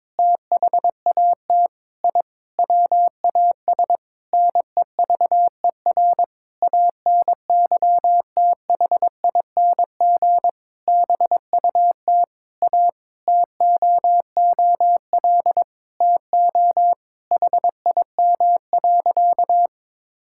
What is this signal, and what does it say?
Morse code, heard as a keyed tone, about 22 wpm, 700 Hz. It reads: THAT I WAS NEVER ANYTHING BUT A TOOL TO HIM.
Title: Sentences from Top 1000 Words - Review Part 6 - 22wpm